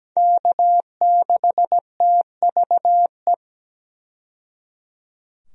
Morse code, heard as a keyed tone, about 17 wpm, 700 Hz. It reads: K6TVE
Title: USA Call Signs - 5 Characters - 17wpm